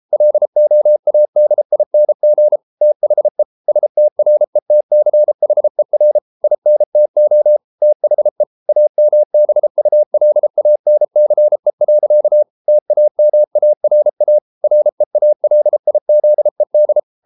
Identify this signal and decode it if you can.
Morse code, heard as a keyed tone, about 33 wpm, 600 Hz. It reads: LOADING THE STRETCHER INTO THE AMBULANCE. TAMARA REALIZED